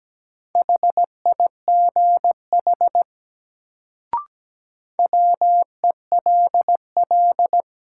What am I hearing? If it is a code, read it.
Morse code, heard as a keyed tone, about 17 wpm, 700 Hz. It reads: HIGH WELL
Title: Sentences from Top 200 Words - New Word Drill - 17wpm